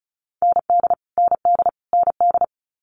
Morse code, heard as a keyed tone, about 36 wpm, 700 Hz. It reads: ND ND ND
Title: Multi-letter Phonemes Repeated 3x 35wpm